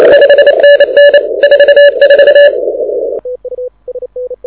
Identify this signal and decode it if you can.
Morse code, heard as a keyed tone, about 28 wpm, 1700 Hz. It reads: 5NN 44
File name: audio zawody / IARU 2015 CW SN7Q